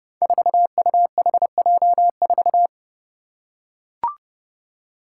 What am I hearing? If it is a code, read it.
Morse code, heard as a keyed tone, about 30 wpm, 700 Hz. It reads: 4UHJ4
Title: Five Challenging Letter-Number Combinations 30wpm